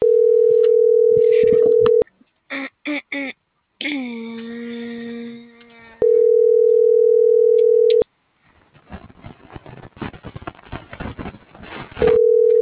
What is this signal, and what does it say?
Ringback tone (North America), 440 + 480 Hz.